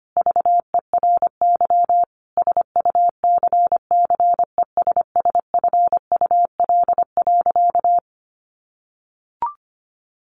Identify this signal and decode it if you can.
Morse code, heard as a keyed tone, about 25 wpm, 700 Hz. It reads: VERY SUCCESSFUL.
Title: Sentences from Top 1300 Words 25wpm